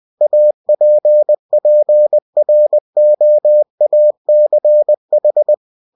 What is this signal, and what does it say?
Morse code, heard as a keyed tone, about 20 wpm, 600 Hz. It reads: APPROACH